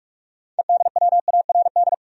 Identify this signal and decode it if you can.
Morse code, heard as a keyed tone, about 46 wpm, 700 Hz. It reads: EDWARD
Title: USA Common Names - 45wpm